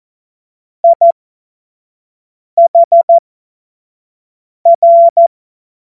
Morse code, heard as a keyed tone, about 14 wpm, spaced as for 5 wpm, 680 Hz. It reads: IHR
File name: Morsekurs / Witze / Witz 04. Mai 2021, 25 F70